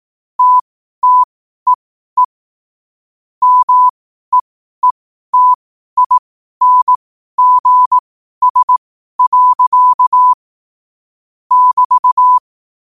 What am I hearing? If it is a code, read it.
Morse code, heard as a keyed tone, about 18 wpm, spaced as for 12 wpm, 1000 Hz. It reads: TTEE MEETINGS. <BT>
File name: qotdfiles / 12wpm